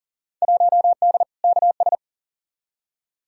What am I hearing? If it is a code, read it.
Morse code, heard as a keyed tone, about 40 wpm, 700 Hz. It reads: 1D KS
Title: Field Day Exchanges - 40wpm